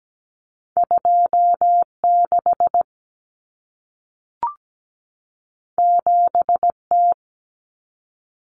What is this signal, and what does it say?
Morse code, heard as a keyed tone, about 17 wpm, 700 Hz. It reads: 26 7T
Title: Common Cut Numbers - 2-Digit 17wpm